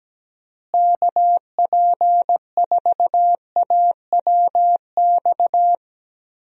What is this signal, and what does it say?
Morse code, heard as a keyed tone, about 17 wpm, 700 Hz. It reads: KP4AWX
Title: DXCC-100 Call Signs - 6 Characters - Repeated 3x - 17wpm